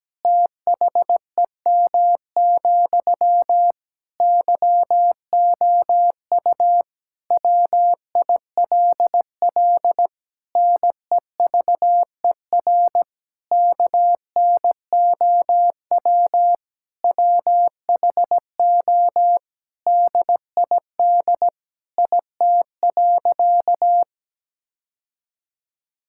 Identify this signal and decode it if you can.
Morse code, heard as a keyed tone, about 17 wpm, 700 Hz. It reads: THEM, YOU WILL NEVER KNOW WHO DID IT.
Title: Sentences from Top 1000 Words - Review Part 6 - 17wpm